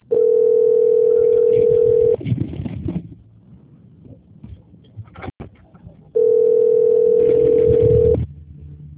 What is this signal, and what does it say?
Ringback tone (North America), 440 + 480 Hz.